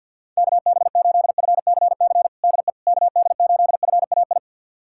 Morse code, heard as a keyed tone, about 52 wpm, 700 Hz. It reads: KB8FCC DE KD8FAI
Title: USA Callsign DE Callsign 50wpm